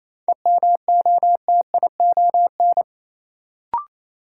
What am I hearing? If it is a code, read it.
Morse code, heard as a keyed tone, about 28 wpm, 700 Hz. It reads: EMOTION
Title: Sentences from Top 1600 Words - New Word Drill - 28wpm